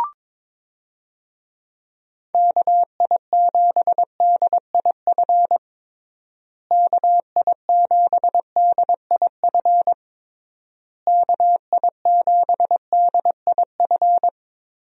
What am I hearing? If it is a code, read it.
Morse code, heard as a keyed tone, about 21 wpm, 700 Hz. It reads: KI7DIF KI7DIF KI7DIF
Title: USA Call Signs - 6 Characters - Repeated 3x - 22wpm